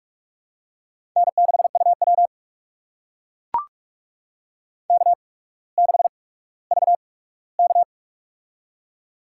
Morse code, heard as a keyed tone, about 45 wpm, spaced as for 19 wpm, 700 Hz. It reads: N6UW K6VK
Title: USA Call Signs - 4 Characters - 8x Character Spacing - 45wpm